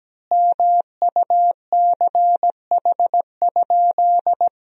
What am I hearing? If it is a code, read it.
Morse code, heard as a keyed tone, about 17 wpm, 700 Hz. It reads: MUCH?
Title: 3x Word-Spacing - Sentences from Top 400 Words - 17wpm